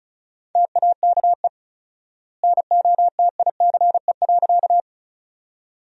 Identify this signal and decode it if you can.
Morse code, heard as a keyed tone, about 36 wpm, 700 Hz. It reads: TAKE NOTICE.
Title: 4x Word-Spacing - Sentences from Top 400 Words - 35wpm